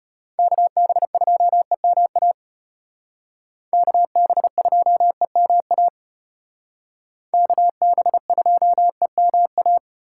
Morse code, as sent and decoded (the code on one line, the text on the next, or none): -.- -... ..--- . -- .- / -.- -... ..--- . -- .- / -.- -... ..--- . -- .-
KB2EMA KB2EMA KB2EMA